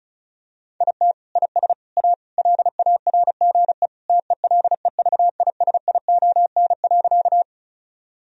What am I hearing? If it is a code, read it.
Morse code, heard as a keyed tone, about 34 wpm, 700 Hz. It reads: IT IS A LARGE TELEVISION.
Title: Sentences from Top 1000 Words - Review Part 4 - 35wpm